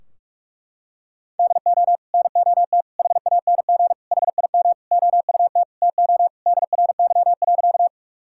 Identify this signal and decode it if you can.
Morse code, heard as a keyed tone, about 46 wpm, 700 Hz. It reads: DO NOT HANG HIM OUT TO DRY.
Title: Sentences from Top 1600 Words 45wpm